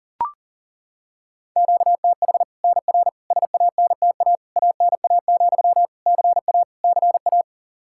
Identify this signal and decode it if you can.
Morse code, heard as a keyed tone, about 40 wpm, 700 Hz. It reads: QTH NR SANTA ANA, CA CA
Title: Practice QSO - Exchange 1 - Beginning - 40wpm